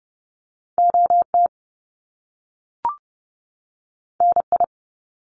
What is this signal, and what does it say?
Morse code, heard as a keyed tone, about 30 wpm, 700 Hz. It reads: OT NI
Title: Top Two Letter Combinations in Words 30wpm